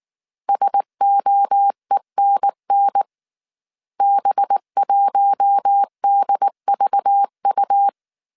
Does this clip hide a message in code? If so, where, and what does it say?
Yes: Morse code from 0.5 to 7.9 s, heard as a keyed tone, about 19 wpm, 800 Hz. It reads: SOENN B1DVU